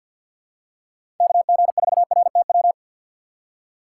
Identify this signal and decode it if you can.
Morse code, heard as a keyed tone, about 52 wpm, 700 Hz. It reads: KG4RTW
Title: USA Call Signs - 6 Characters - 4x Character Spacing - 50wpm